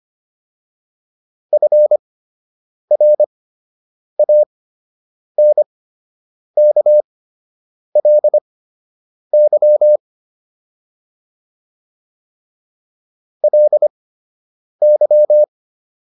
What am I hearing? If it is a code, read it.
Morse code, heard as a keyed tone, about 25 wpm, spaced as for 8 wpm, 600 Hz. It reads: FRANKLY LY